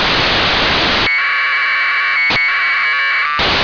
Packet radio (AX.25, as in APRS).
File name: antelsat 0843utc